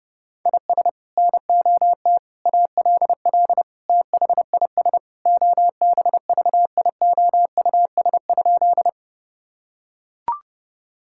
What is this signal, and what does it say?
Morse code, heard as a keyed tone, about 30 wpm, 700 Hz. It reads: IS NOT ALL THIS OBVIOUS?
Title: Sentences from Top 1600 Words 30wpm